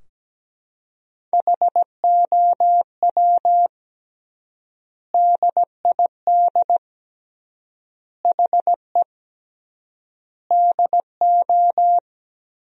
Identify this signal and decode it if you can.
Morse code, heard as a keyed tone, about 17 wpm, 700 Hz. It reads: HOW DID HE DO
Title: 3x Word-Spacing - Sentences from Top 100 Words - 17wpm